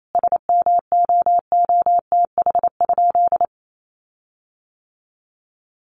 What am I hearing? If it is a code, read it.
Morse code, heard as a keyed tone, about 28 wpm, 700 Hz. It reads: SMOOTH?
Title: Sentences from Top 1400 Words 28wpm